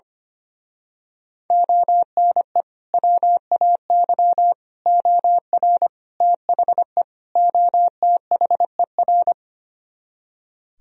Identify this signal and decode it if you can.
Morse code, heard as a keyed tone, about 25 wpm, 700 Hz. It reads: ONE WAY OR THE OTHER
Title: Speed-Racing - Sets of 5 Words 25wpm